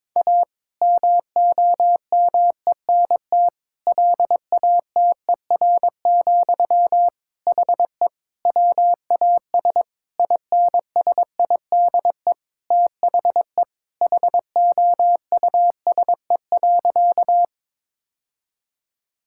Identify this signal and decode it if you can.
Morse code, heard as a keyed tone, about 21 wpm, 700 Hz. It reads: A MOMENT LATER, HE WAS INSIDE THE HOUSE.
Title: Sentences from Top 1000 Words - Review Part 6 - 22wpm